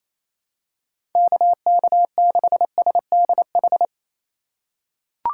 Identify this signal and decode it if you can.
Morse code, heard as a keyed tone, about 28 wpm, 700 Hz. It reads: KK6SDH
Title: USA Call Signs - 6 Characters - 4x Character Spacing - 28wpm